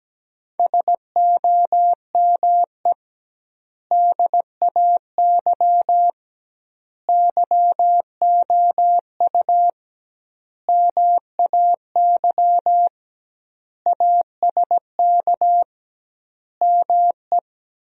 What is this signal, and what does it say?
Morse code, heard as a keyed tone, about 17 wpm, 700 Hz. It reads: SOME DAY YOU MAY ASK ME
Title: 2x Word-Spacing - Sentences from Top 200 Words - 17wpm